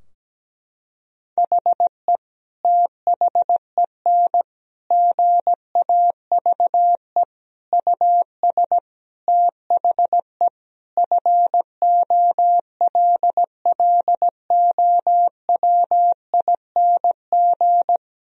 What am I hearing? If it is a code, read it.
Morse code, heard as a keyed tone, about 17 wpm, 700 Hz. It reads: HE THEN GAVE US THE FOLLOWING
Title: Sentences from Top 1000 Words - Review Part 4 - 17wpm